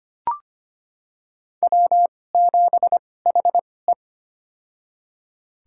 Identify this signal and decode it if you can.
Morse code, heard as a keyed tone, about 25 wpm, spaced as for 18 wpm, 700 Hz. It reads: W7HE
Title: USA Call Signs - 4 Characters - 2x Character Spacing - 25wpm